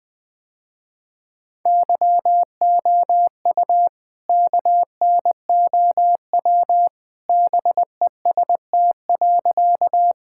Morse code, heard as a keyed tone, about 20 wpm, 700 Hz. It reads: YOU KNOW BEST.
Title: Sentences with 2-Word Phrases - Spoken Beginning - 20wpm